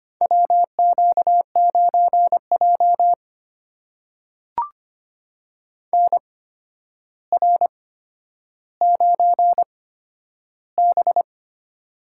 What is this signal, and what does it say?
Morse code, heard as a keyed tone, about 25 wpm, spaced as for 11 wpm, 700 Hz. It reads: WQ9J NR9B
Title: USA Call Signs - 4 Characters - 8x Character Spacing - 25wpm